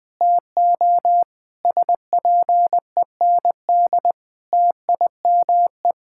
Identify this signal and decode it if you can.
Morse code, heard as a keyed tone, about 20 wpm, 700 Hz. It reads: TO SPEND TIME
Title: Sentences from Top 1000 Words - Review Part 6 - 20wpm